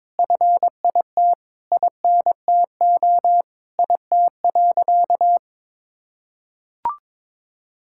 Morse code, heard as a keyed tone, about 22 wpm, 700 Hz. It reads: FIT INTO IT.
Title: Sentences from Top 1000 Words - Review Part 5 - 22wpm